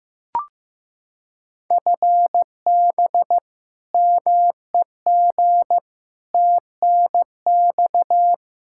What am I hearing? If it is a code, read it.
Morse code, heard as a keyed tone, about 15 wpm, 700 Hz. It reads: FB MEG TNX
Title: Practice QSO - Exchange 3 - More Info - 15wpm